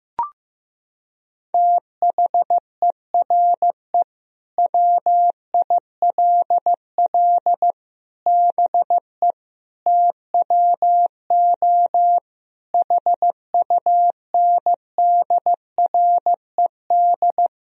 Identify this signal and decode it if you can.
Morse code, heard as a keyed tone, about 16 wpm, 700 Hz. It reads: THERE WILL BE TWO HUNDRED
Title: Sentences from Top 1000 Words - Review Part 3 - 15wpm